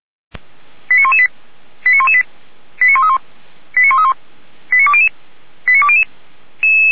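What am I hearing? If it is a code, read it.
Five-tone selective call (ZVEI), heard as quick runs of five tones, tones of about 70 ms. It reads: 89118 89118 89131 89131 89200 89200
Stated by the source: ZVEI five-tone call 89131, received by SDR.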